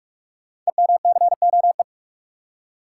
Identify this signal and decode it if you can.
Morse code, heard as a keyed tone, about 44 wpm, 700 Hz. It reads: EMCOE
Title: Code Groups - 5 Letter:Numbers - 45wpm